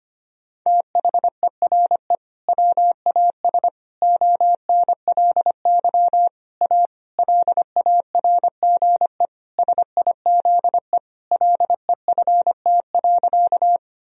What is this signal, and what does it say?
Morse code, heard as a keyed tone, about 25 wpm, 700 Hz. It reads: THERE WAS ONLY A LARGE SIZE LEFT.
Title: Sentences from Top 1000 Words - Review Part 2 - 25wpm